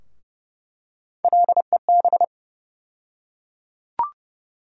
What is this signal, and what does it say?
Morse code, heard as a keyed tone, about 30 wpm, 700 Hz. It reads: LEB
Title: USA Call Sign Postfixes - 3 Letters 30wpm